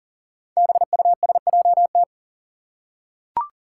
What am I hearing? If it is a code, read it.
Morse code, heard as a keyed tone, about 40 wpm, 700 Hz. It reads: BUSJT
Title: Five Challenging Letter-Number Combinations 40wpm